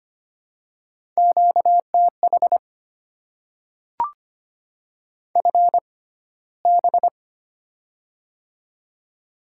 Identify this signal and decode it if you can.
Morse code, heard as a keyed tone, about 25 wpm, spaced as for 15 wpm, 700 Hz. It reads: QTH FB
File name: Top 20 QSO Elements - 6x Character Spacing - 25wpm